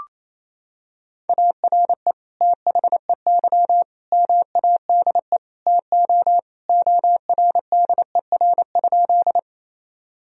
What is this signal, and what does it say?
Morse code, heard as a keyed tone, about 27 wpm, 700 Hz. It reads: ARE THEY MADE TO ORDER?
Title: Sentences from Top 1000 Words - Review Part 2 - 28wpm